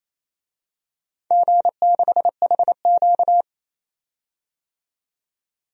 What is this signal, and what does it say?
Morse code, heard as a keyed tone, about 28 wpm, 700 Hz. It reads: G6HQ
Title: DXCC-100 Call Signs - 4 Characters - 28wpm